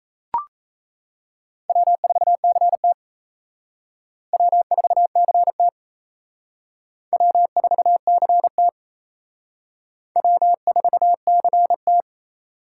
Morse code, heard as a keyed tone, about 37 wpm, 700 Hz. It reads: W4CT W4CT W4CT W4CT